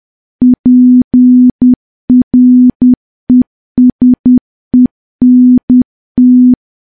Morse code, heard as a keyed tone, about 10 wpm, 250 Hz. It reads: PRESENT
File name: morse / morse-training-10WPM